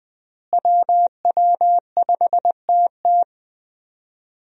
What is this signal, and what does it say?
Morse code, heard as a keyed tone, about 20 wpm, 700 Hz. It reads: WW5TT
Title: USA Call Signs - 5 Characters - Repeated 3x - 20wpm